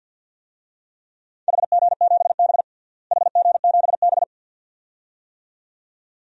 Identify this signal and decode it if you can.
Morse code, heard as a keyed tone, about 50 wpm, 700 Hz. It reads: HG7B HG7B
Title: DXCC-100 Call Signs - 4 Characters - Repeated 2x - 50wpm